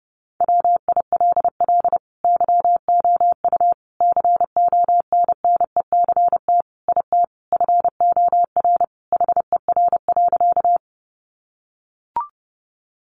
Morse code, heard as a keyed tone, about 30 wpm, 700 Hz. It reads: WILL YOU CONNECT IT FOR HER.
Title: Sentences from Top 1500 Words 30wpm